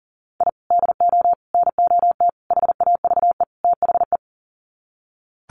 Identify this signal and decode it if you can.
Morse code, heard as a keyed tone, about 40 wpm, 700 Hz. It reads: I DO NOT HAVE THE
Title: Sets of 5 Words 40wpm